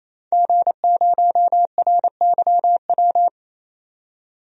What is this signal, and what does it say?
Morse code, heard as a keyed tone, about 28 wpm, 700 Hz. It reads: G0RYW
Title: DXCC-100 Call Signs - 5 Characters - 28wpm